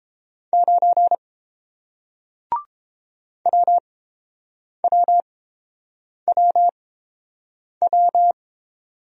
Morse code, heard as a keyed tone, about 30 wpm, spaced as for 7 wpm, 700 Hz. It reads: 9 WWWW